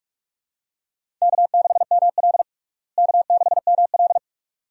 Morse code, heard as a keyed tone, about 42 wpm, 700 Hz. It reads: K6ML K6ML
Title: USA Call Signs - 4 Characters - Repeated 2x - 45wpm